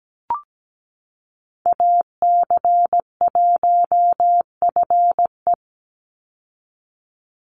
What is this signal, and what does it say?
Morse code, heard as a keyed tone, about 17 wpm, 700 Hz. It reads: AC1FE
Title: Call Signs Repeated 3x - 17wpm